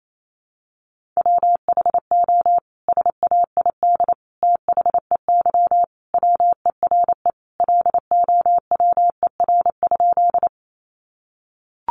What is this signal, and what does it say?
Morse code, heard as a keyed tone, about 28 wpm, 700 Hz. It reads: WHO SAID THEY WERE LOWER?
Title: Sentences from Top 700 Words 28wpm